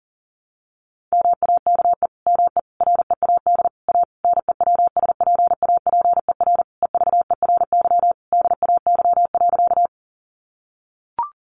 Morse code, heard as a keyed tone, about 40 wpm, 700 Hz. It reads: MAKE ME READ A NEWSPAPER EVERY DAY.
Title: Sentences from Top 1100 Words 40wpm